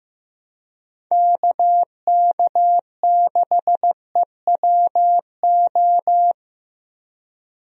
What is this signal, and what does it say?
Morse code, heard as a keyed tone, about 15 wpm, 700 Hz. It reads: KK6EWO
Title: USA Call Signs - 6 Characters - Repeated 2x - 15wpm